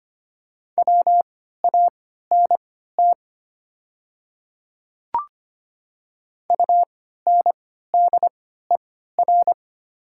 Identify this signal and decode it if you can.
Morse code, heard as a keyed tone, about 25 wpm, spaced as for 14 wpm, 700 Hz. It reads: WANT UNDER
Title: 3x Character Spacing - Top 200 Words - 25wpm